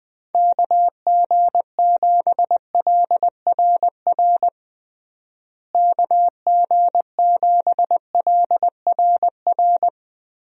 Morse code, heard as a keyed tone, about 20 wpm, 700 Hz. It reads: KG7LRR KG7LRR
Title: USA Call Signs - 6 Characters - Repeated 3x - 20wpm